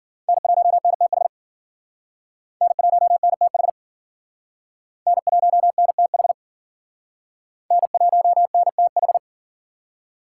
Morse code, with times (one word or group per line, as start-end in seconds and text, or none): 0.3-1.3 N1NTH
2.6-3.7 N1NTH
5.1-6.3 N1NTH
7.7-9.2 N1NTH